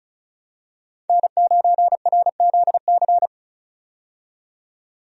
Morse code, heard as a keyed tone, about 36 wpm, 700 Hz. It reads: N9RZC
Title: USA Call Signs - 5 Characters - 35wpm